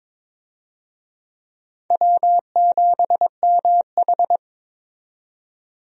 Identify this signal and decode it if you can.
Morse code, heard as a keyed tone, about 22 wpm, 700 Hz. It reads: W7MH
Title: USA Call Signs - 4 Characters - 22wpm